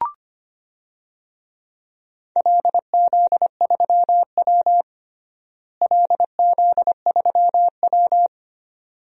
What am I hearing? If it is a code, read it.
Morse code, heard as a keyed tone, about 25 wpm, 700 Hz. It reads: LZ3W LZ3W
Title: DXCC-100 Call Signs - 4 Characters - Repeated 2x - 25wpm